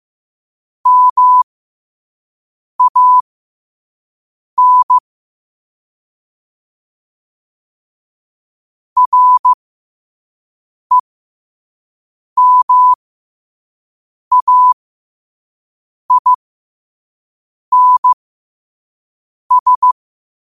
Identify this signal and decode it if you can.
Morse code, heard as a keyed tone, about 15 wpm, spaced as for 5 wpm, 1000 Hz. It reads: MAN REMAINS
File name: qotdfiles / 5wpm